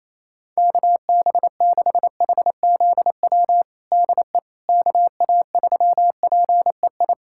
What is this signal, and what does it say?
Morse code, heard as a keyed tone, about 28 wpm, 700 Hz. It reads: KB6HZW DE KA3PEI
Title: USA Callsign DE Callsign - 6 Characters - 28wpm